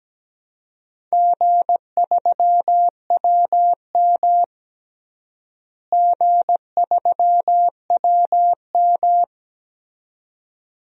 Morse code, heard as a keyed tone, about 17 wpm, 700 Hz. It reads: G3WM G3WM
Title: DXCC-100 Call Signs - 4 Characters - Repeated 2x - 17wpm